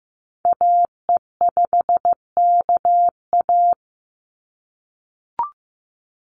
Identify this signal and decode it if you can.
Morse code, heard as a keyed tone, about 15 wpm, 700 Hz. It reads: AE5KA
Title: NA SOTA Activator Callsigns 15wpm